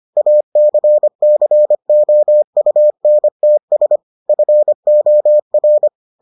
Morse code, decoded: ACCOUNTS FOR

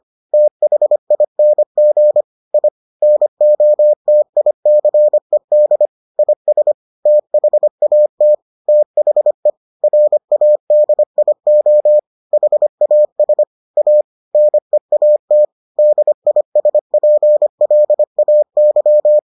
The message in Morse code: THING I NOTICED IS THAT THE RADIO HAS A NEAT DISPLAY

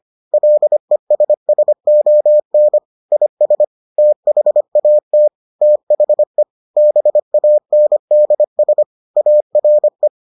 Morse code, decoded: LESSON IS THAT THE BANDS ARE

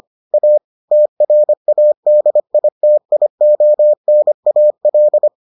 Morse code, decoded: A TRADITIONAL

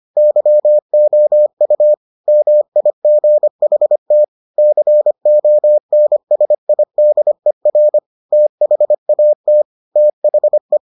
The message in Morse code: YOU MIGHT CONSIDER THAT THE

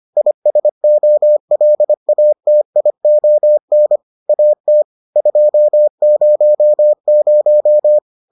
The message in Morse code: ISOLATION AT 200

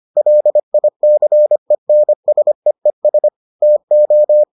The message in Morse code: LICENSEES TO

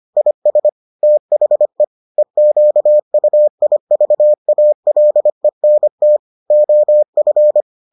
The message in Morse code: IS THE EQUIVALENT OF

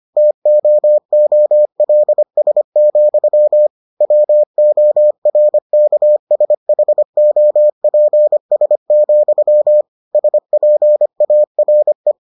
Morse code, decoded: TOOLS, WORKSHOPS, SPARE

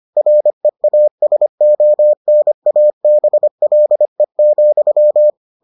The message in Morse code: REASONABLE,

